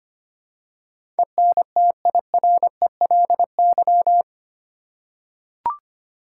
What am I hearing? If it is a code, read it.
Morse code, heard as a keyed tone, about 25 wpm, 700 Hz. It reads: ENTIRELY
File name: Eight Letter Words - 25wpm